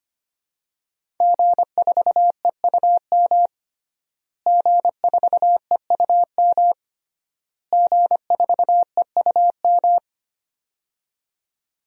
Morse code, heard as a keyed tone, about 25 wpm, 700 Hz. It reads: G4EUM G4EUM G4EUM
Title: DXCC-100 Call Signs - 5 Characters - Repeated 3x - 25wpm